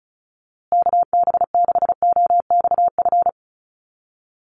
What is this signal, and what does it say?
Morse code, heard as a keyed tone, about 34 wpm, 700 Hz. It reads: KB6OXF
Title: USA Call Signs Repeated 2x 35wpm